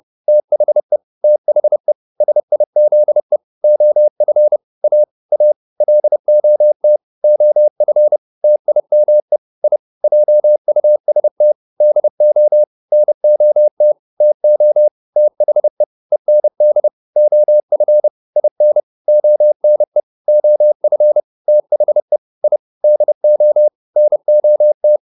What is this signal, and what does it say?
Morse code, heard as a keyed tone, about 30 wpm, 600 Hz. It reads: THE THE SIZE OF A A LOT OF TIME I JUST DO NOT TO THE END OF IN ONE OF THE I DO NOT